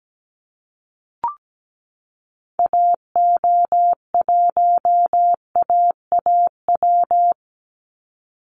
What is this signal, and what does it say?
Morse code, heard as a keyed tone, about 17 wpm, 700 Hz. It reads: AO1AAW
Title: Call Signs - 17wpm